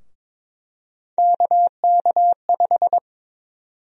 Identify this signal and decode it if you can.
Morse code, heard as a keyed tone, about 22 wpm, 700 Hz. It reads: KK5
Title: USA Call Sign Prefixes - 22wpm